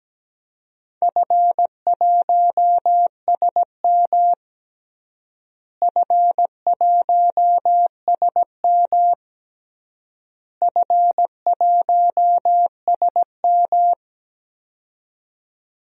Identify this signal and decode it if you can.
Morse code, heard as a keyed tone, about 17 wpm, 700 Hz. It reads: F1SM F1SM F1SM
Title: DXCC-100 Call Signs - 4 Characters - Repeated 3x - 17wpm